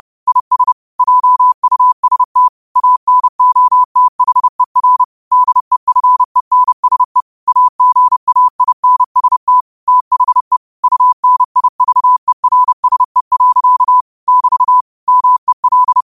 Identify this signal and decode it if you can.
Morse code, heard as a keyed tone, about 30 wpm, 1000 Hz. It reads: IS JUST ANOTHER DEFENSE AGAINST THE UNIVERSE. <BT> MEL
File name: qotdfiles / 30wpm